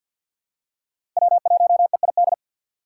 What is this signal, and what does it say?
Morse code, heard as a keyed tone, about 52 wpm, 700 Hz. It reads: W1EID
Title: USA Call Signs - 5 Characters - 4x Character Spacing - 50wpm